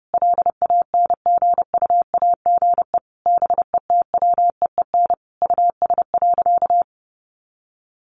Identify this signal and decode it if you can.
Morse code, heard as a keyed tone, about 30 wpm, 700 Hz. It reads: LANGUAGE BETWEEN US.
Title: Sentences from Top 500 Words 30wpm